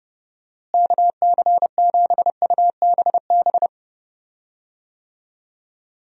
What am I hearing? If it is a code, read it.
Morse code, heard as a keyed tone, about 30 wpm, 700 Hz. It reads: KC7UBB
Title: USA Call Signs - 6 Characters - Repeated 2x - 30wpm